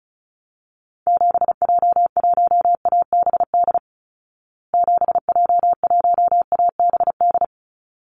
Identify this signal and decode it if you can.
Morse code, heard as a keyed tone, about 34 wpm, 700 Hz. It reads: 7J1ABD 7J1ABD
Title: Call Signs Repeated 3x - 35wpm